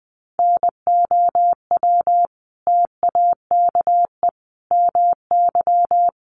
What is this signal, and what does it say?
Morse code, heard as a keyed tone, about 20 wpm, 700 Hz. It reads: NOW TAKE MY
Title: Sentences from Top 1100 Words 20wpm